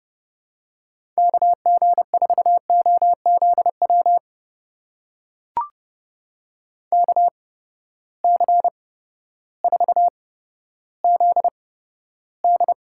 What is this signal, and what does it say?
Morse code, heard as a keyed tone, about 30 wpm, spaced as for 14 wpm, 700 Hz. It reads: KG4OZW KC4ZD
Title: USA Call Signs - 6 Characters - 8x Character Spacing - 30wpm